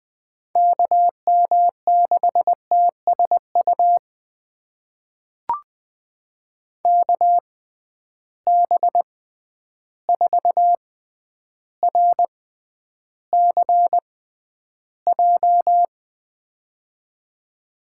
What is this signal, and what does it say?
Morse code, heard as a keyed tone, about 20 wpm, spaced as for 10 wpm, 700 Hz. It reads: KM6TSU KB4RCJ
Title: USA Call Signs - 6 Characters - 6x Character Spacing - 20wpm